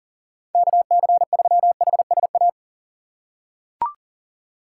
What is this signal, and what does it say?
Morse code, heard as a keyed tone, about 40 wpm, 700 Hz. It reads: KC3HSA
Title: USA Call Signs - 6 Characters - 3x Character Spacing - 40wpm